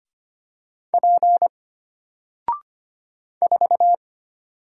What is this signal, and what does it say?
Morse code, heard as a keyed tone, about 25 wpm, 700 Hz. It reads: P 4
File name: Single-Letter-Number-25wpm